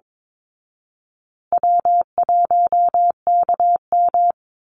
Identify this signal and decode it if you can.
Morse code, heard as a keyed tone, about 22 wpm, 700 Hz. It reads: W1KM
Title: Easy Call Signs Repeated 3x - 22wpm